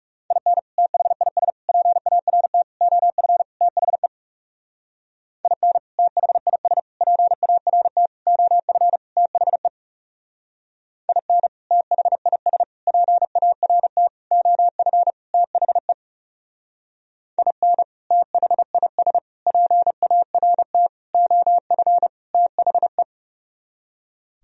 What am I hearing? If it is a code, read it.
Morse code, heard as a keyed tone, about 37 wpm, 700 Hz. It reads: IN THIS PART OF THE IN THIS PART OF THE IN THIS PART OF THE IN THIS PART OF THE
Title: Speed-Racing - Sets of 5 Words 30wpm